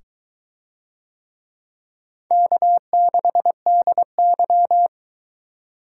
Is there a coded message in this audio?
Yes: Morse code from 2.3 to 4.9 s, heard as a keyed tone, about 22 wpm, 700 Hz. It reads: K6DY